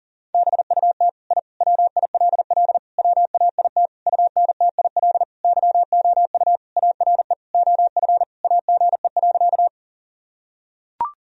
Morse code, heard as a keyed tone, about 37 wpm, 700 Hz. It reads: BUT I WILL WAIT UNTIL YOU ARE OF AGE.
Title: Sentences from Top 1000 Words - Review Part 3 - 40wpm